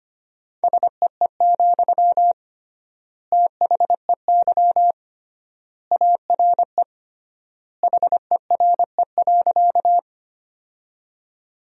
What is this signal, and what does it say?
Morse code, heard as a keyed tone, about 25 wpm, 700 Hz. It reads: SEE, THEY ARE HERE.
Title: 3x Word-Spacing - Sentences from Top 200 Words - 25wpm